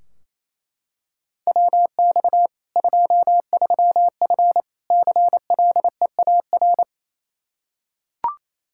Morse code, heard as a keyed tone, about 28 wpm, 700 Hz. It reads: WX 23F CLEAR
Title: K1USN SST QSO Practice 28wpm